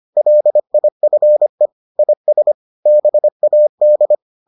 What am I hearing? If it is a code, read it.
Morse code, heard as a keyed tone, about 25 wpm, 600 Hz. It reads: LIFE IS BAD